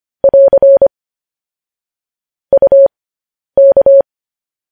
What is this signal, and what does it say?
Morse code, heard as a keyed tone, about 25 wpm, spaced as for 10 wpm, 550 Hz. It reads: <AR> UK
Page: news headlines in morse code at 10 wpm